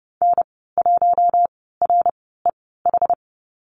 Morse code, heard as a keyed tone, about 29 wpm, spaced as for 17 wpm, 700 Hz. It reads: N1REH
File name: Lesson 14g - Number 2 - Call Signs 30wpm